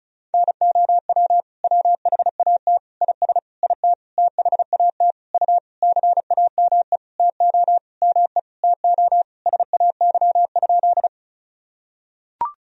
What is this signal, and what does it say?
Morse code, heard as a keyed tone, about 35 wpm, 700 Hz. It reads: NOW WHAT IS IT THAT U CAME TO ME TO SAY?
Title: Sentences from Top 200 Words and Abbreviations 35wpm